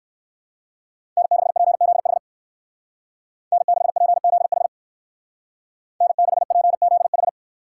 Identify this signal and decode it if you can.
Morse code, heard as a keyed tone, about 55 wpm, 700 Hz. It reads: N6PZH N6PZH N6PZH
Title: Speed Racing - USA 1x3 Call Signs 45wpm